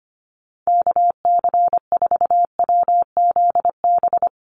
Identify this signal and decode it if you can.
Morse code, heard as a keyed tone, about 25 wpm, 700 Hz. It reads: KC4WZB
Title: NA SOTA Activator Callsigns 25wpm